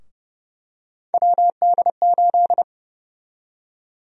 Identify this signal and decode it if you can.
Morse code, heard as a keyed tone, about 30 wpm, 700 Hz. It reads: WD8
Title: USA Call Sign Prefixes - 30wpm